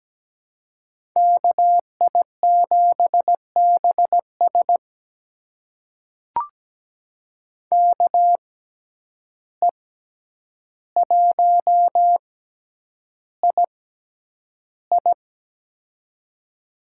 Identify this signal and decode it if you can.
Morse code, heard as a keyed tone, about 17 wpm, spaced as for 9 wpm, 700 Hz. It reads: KI7BS KE1II
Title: USA Call Signs - 5 Characters - 6x Character Spacing - 17wpm